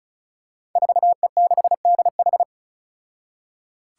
Morse code, heard as a keyed tone, about 36 wpm, 700 Hz. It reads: 4E6DH
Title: Five Challenging Letter-Number Combinations 35wpm